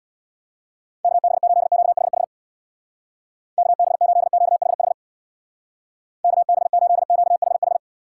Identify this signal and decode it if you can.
Morse code, heard as a keyed tone, about 64 wpm, 700 Hz. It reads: KB8QHH KB8QHH KB8QHH